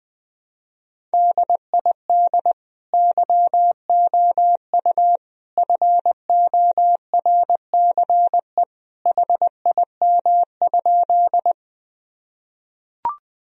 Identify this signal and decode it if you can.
Morse code, heard as a keyed tone, about 20 wpm, 700 Hz. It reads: DID YOU FORCE HIM?
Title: Sentences from Top 1000 Words - Review Part 3 - 20wpm